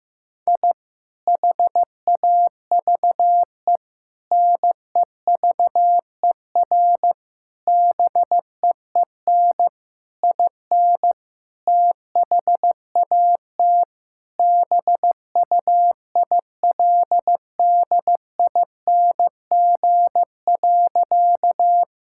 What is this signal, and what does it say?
Morse code, heard as a keyed tone, about 15 wpm, 700 Hz. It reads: I HAVE NEVER BEEN IN THAT BUILDING.
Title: Sentences from Top 1000 Words - Review Part 5 - 15wpm